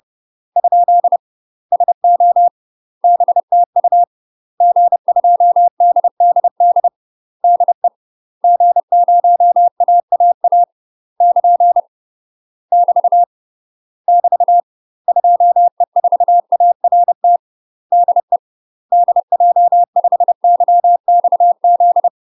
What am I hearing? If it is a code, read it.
Morse code, heard as a keyed tone, about 30 wpm, 700 Hz. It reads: ? SO BTU G2DDD DE G0AAA <KN> <BT> <BT> 2E4ART DE DJ5YXZ
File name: UK MASTER QSOs TRAINING / 30WPM / UK MASTER QSOs 0027